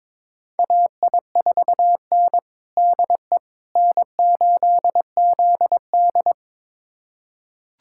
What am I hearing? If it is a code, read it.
Morse code, heard as a keyed tone, about 23 wpm, 700 Hz. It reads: AI4N DE N8ZD
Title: USA Callsign DE Callsign - 4 Characters - 22wpm